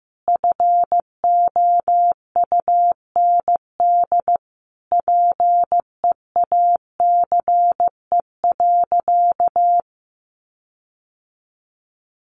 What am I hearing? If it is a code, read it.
Morse code, heard as a keyed tone, about 15 wpm, 700 Hz. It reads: FOUND PEACE.
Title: Sentences from Top 1500 Words 15wpm